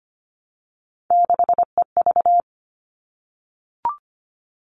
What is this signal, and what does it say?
Morse code, heard as a keyed tone, about 25 wpm, 700 Hz. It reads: 6EV
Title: All Cut Numbers - 3-Digit 25wpm